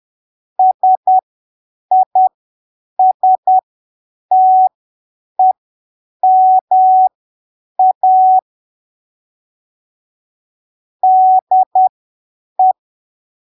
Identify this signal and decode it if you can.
Morse code, heard as a keyed tone, about 10 wpm, spaced as for 7 wpm, 750 Hz. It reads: SISTEMA DE
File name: arquivos / texto / texto 09 10ppm